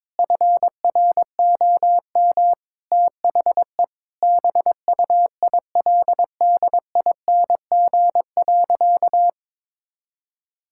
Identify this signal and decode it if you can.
Morse code, heard as a keyed tone, about 22 wpm, 700 Hz. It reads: FROM THE BUILDING.
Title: Sentences from Top 1000 Words - Review Part 5 - 22wpm